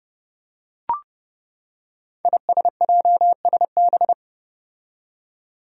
Morse code, heard as a keyed tone, about 31 wpm, 700 Hz. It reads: ISJSB
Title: Five Challenging Letter-Number Combinations 30wpm